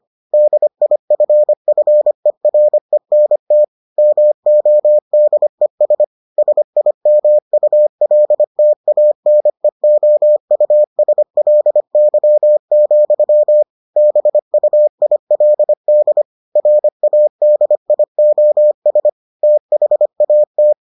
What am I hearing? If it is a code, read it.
Morse code, heard as a keyed tone, about 25 wpm, 600 Hz. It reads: DIFFERENT MODES SIMULTANEOUSLY, BUILD RADIOS THAT